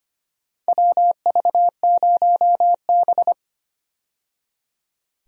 Morse code, heard as a keyed tone, about 25 wpm, 700 Hz. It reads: WV0B
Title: USA Call Signs - 4 Characters - 25wpm